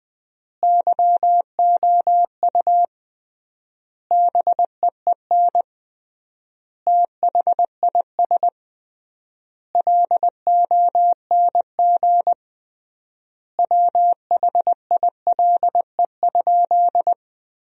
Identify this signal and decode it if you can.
Morse code, heard as a keyed tone, about 20 wpm, 700 Hz. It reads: YOU BEEN THIS LONG WHILE?
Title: 3x Word-Spacing - Sentences from Top 300 Words - 20wpm